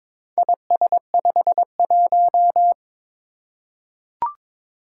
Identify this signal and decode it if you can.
Morse code, heard as a keyed tone, about 22 wpm, 700 Hz. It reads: IS51
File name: practice / 22wpm / Four Letter-Number Combinations - 22wpm